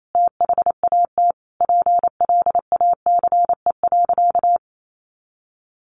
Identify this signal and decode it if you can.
Morse code, heard as a keyed tone, about 28 wpm, 700 Hz. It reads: THAT PLACE.
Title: Sentences from Top 800 Words 28wpm